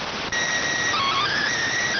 SSTV (slow-scan television), PD 290.